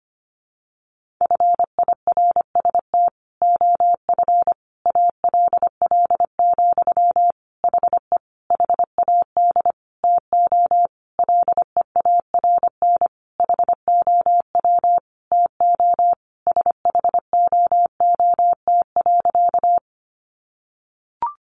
Morse code, heard as a keyed tone, about 25 wpm, 700 Hz. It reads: FIRST OF ALL, HE HAD TO LEARN HOW TO SHOOT.
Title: Sentences from Top 1200 Words 25wpm